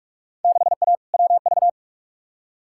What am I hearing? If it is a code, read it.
Morse code, heard as a keyed tone, about 46 wpm, 700 Hz. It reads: 6A WV
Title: Field Day Exchanges - 45wpm